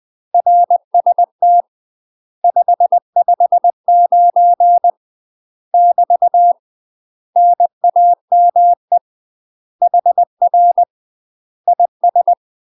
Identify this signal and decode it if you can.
Morse code, heard as a keyed tone, about 20 wpm, 700 Hz. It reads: RST 559 <BT> NAME HR IS
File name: UK MASTER QSOs TRAINING / 20WPM / UK MASTER QSOs 0062